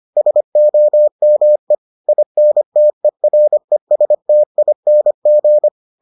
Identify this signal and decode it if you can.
Morse code, heard as a keyed tone, about 25 wpm, 600 Hz. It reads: SOME INTERESTING